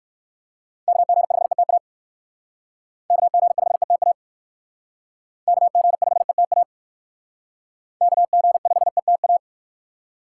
Morse code, heard as a keyed tone, about 56 wpm, 700 Hz. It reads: KG5ETA KG5ETA KG5ETA KG5ETA
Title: Speed Racing - USA Call Signs 45wpm